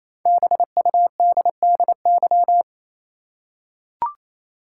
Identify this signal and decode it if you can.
Morse code, heard as a keyed tone, about 28 wpm, 700 Hz. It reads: BUDDY
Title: Five Letter Words - 28wpm